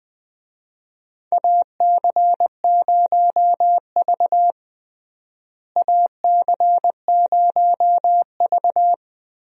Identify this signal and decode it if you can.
Morse code, heard as a keyed tone, about 21 wpm, 700 Hz. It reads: AC0V AC0V
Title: USA Call Signs - 4 Characters - Repeated 3x - 20wpm